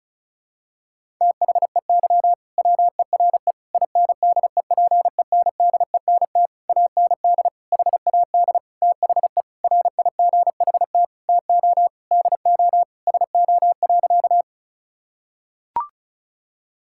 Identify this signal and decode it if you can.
Morse code, heard as a keyed tone, about 36 wpm, 700 Hz. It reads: THEY WERE INDEPENDENT AND HAD THE RIGHT TO DO SO.
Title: Sentences from Top 1000 Words - Review Part 6 - 35wpm